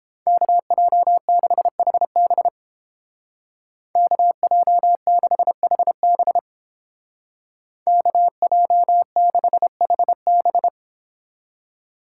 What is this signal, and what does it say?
Morse code, heard as a keyed tone, about 31 wpm, 700 Hz. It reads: KJ6HB KJ6HB KJ6HB